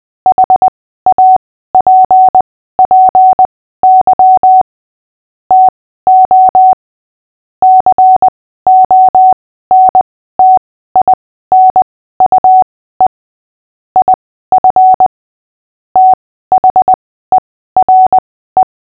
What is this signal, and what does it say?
Morse code, heard as a keyed tone, about 20 wpm, spaced as for 14 wpm, 750 Hz. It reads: HAPPY TO CONTINUE IF THERE
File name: cw / fists / Short-Story-8-14wpm